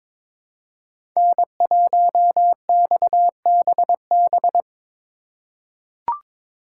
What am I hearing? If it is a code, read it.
Morse code, heard as a keyed tone, about 22 wpm, 700 Hz. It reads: N1XBB
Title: USA Call Signs - 5 Characters - 2x Character Spacing - 22wpm